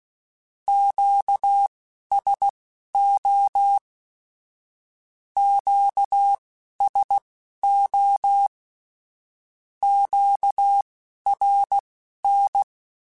Morse code, heard as a keyed tone, about 16 wpm, spaced as for 11 wpm, 800 Hz. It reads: QSO QSO QRN
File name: (spaced) / Q-Calls Spaced-mp3s / Q Calls Twice sp @15